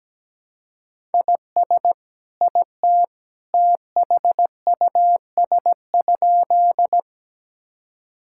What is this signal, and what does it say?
Morse code, heard as a keyed tone, about 17 wpm, 700 Hz. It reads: IS IT THUS?
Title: Sentences from Top 1000 Words - Review Part 5 - 17wpm